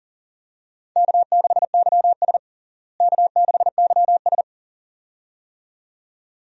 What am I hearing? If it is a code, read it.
Morse code, heard as a keyed tone, about 40 wpm, 700 Hz. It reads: K6YS K6YS
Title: USA Call Signs - 4 Characters - Repeated 2x - 40wpm